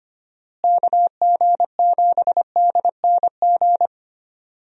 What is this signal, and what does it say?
Morse code, heard as a keyed tone, about 25 wpm, 700 Hz. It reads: KG7DNG
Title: USA Call Signs - 6 Characters - Repeated 2x - 25wpm